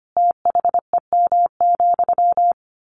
Morse code, heard as a keyed tone, about 25 wpm, 700 Hz. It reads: THEM,
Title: A Sound Of Thunder 25wpm